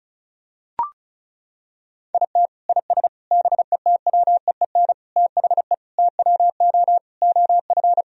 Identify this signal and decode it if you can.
Morse code, heard as a keyed tone, about 36 wpm, 700 Hz. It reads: IT IS BETWEEN THE TWO OF
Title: Sentences from Top 1000 Words - Review Part 1 - 35wpm